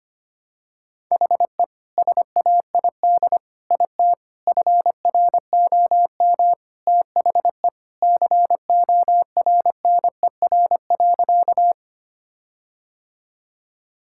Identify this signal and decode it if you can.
Morse code, heard as a keyed tone, about 25 wpm, 700 Hz. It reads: HE SAID IT FROM THE CORNER.
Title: Sentences from Top 1600 Words 25wpm